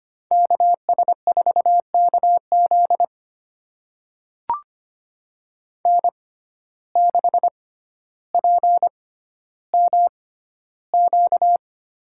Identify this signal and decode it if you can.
Morse code, heard as a keyed tone, about 25 wpm, spaced as for 13 wpm, 700 Hz. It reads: KS4KZ N6PMQ
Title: USA Call Signs - 5 Characters - 6x Character Spacing - 25wpm